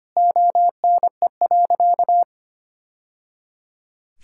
Morse code, heard as a keyed tone, about 25 wpm, 700 Hz. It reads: ONE.
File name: practice / 25wpm / Sentences with 2-Word Phrases - 25wpm